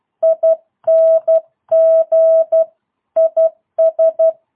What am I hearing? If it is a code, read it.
Morse code, heard as a keyed tone, about 12 wpm, 650 Hz. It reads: ING IS
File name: jh8cbh / 2022cw